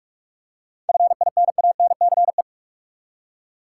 Morse code, heard as a keyed tone, about 42 wpm, 700 Hz. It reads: FINANCE
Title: Seven Letter Words - 45wpm